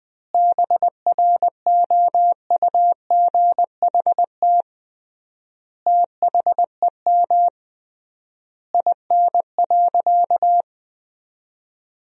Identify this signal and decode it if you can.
Morse code, heard as a keyed tone, about 20 wpm, 700 Hz. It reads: BROUGHT THEM IN.
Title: 3x Word-Spacing - Sentences from Top 500 Words - 20wpm